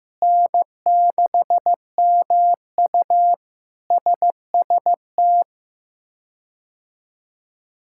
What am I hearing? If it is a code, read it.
Morse code, heard as a keyed tone, about 15 wpm, 700 Hz. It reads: N6MU SST
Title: K1USN SST QSO Practice - 15wpm